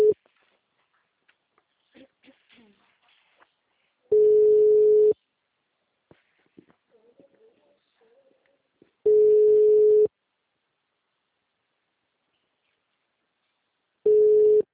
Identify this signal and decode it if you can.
Ringback tone (Europe, 425 Hz), with speech.
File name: out- -2021- - - .10937